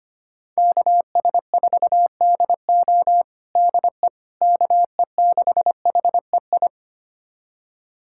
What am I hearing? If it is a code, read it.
Morse code, heard as a keyed tone, about 25 wpm, 700 Hz. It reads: KS4DO DE KE6HEI
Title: USA Callsign DE Callsign 25wpm